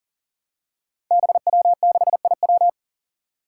Morse code, heard as a keyed tone, about 41 wpm, 700 Hz. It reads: BW6IW
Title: Five Challenging Letter-Number Combinations 40wpm